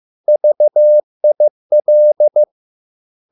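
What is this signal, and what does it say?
Morse code, heard as a keyed tone, about 15 wpm, 600 Hz. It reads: VIL
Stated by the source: Morse code at a slow 15 wpm from CW med Gnister.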